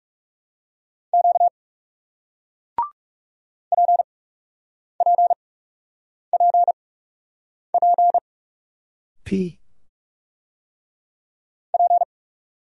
Morse code, heard as a keyed tone, about 38 wpm, spaced as for 8 wpm, 700 Hz. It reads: Q PPPP P